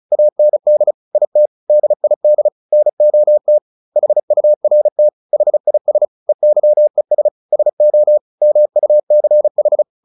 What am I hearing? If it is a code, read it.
Morse code, heard as a keyed tone, about 34 wpm, 600 Hz. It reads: AND IT DID NOT HURT HIS EYES SO MUCH
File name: the old man and the sea / the old man and the sea 1 35wpm